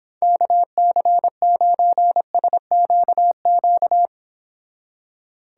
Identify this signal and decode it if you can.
Morse code, heard as a keyed tone, about 26 wpm, 700 Hz. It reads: KC9SQQ